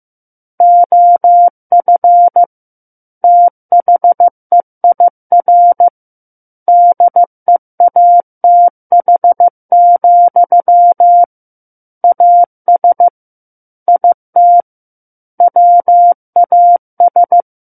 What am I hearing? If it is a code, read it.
Morse code, heard as a keyed tone, about 15 wpm, 700 Hz. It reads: OF THEIR DEATH, AS IT WAS
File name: Book One / 06-Chapter 6 - 15 WPM